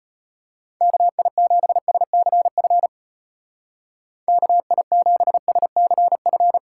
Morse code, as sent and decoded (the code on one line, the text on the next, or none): -.- .. --... ... -.-. ..-. / -.- .. --... ... -.-. ..-.
KI7SCF KI7SCF